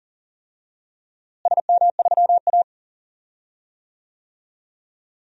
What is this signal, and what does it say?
Morse code, heard as a keyed tone, about 40 wpm, 700 Hz. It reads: SM3A
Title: DXCC-100 Call Signs - 4 Characters - Repeated 3x - 40wpm